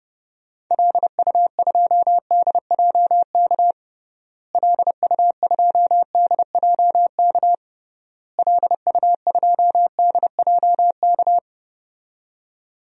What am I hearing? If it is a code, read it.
Morse code, heard as a keyed tone, about 32 wpm, 700 Hz. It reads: LU2DJK LU2DJK LU2DJK
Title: DXCC-100 Call Signs - 6 Characters - Repeated 3x - 30wpm